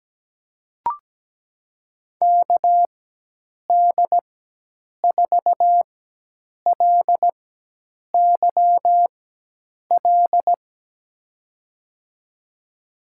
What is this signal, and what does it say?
Morse code, heard as a keyed tone, about 17 wpm, spaced as for 8 wpm, 700 Hz. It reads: KD4LYL
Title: USA Call Signs - 6 Characters - 4x Character Spacing - 17wpm